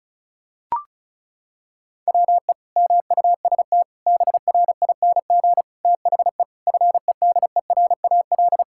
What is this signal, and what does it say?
Morse code, heard as a keyed tone, about 34 wpm, 700 Hz. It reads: WE MUST BRING THE FEDERAL
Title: Sentences from Top 1000 Words - Review Part 6 - 35wpm